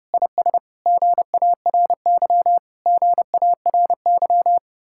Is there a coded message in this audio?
Yes: Morse code from 0.1 to 4.6 s, heard as a keyed tone, about 33 wpm, 700 Hz. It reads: IS GARY GARY